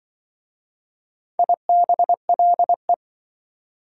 Morse code, heard as a keyed tone, about 24 wpm, 700 Hz. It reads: IBLE
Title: Speed-Racing Prefixes and Suffixes 20wpm